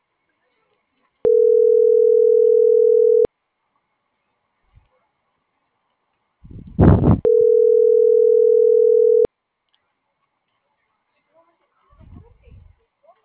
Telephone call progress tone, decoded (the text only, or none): ringback tone